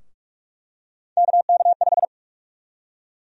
Morse code, heard as a keyed tone, about 46 wpm, 700 Hz. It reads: KK5
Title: USA Call Sign Prefixes - 45wpm